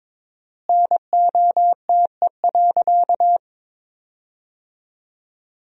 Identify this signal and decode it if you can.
Morse code, heard as a keyed tone, about 22 wpm, 700 Hz. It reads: NOTE.
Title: 4x Word-Spacing - Sentences from Top 500 Words - 22wpm